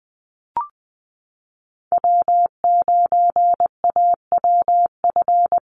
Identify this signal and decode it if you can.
Morse code, heard as a keyed tone, about 20 wpm, 700 Hz. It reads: W9AWF
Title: USA Call Signs Repeated 2x 20wpm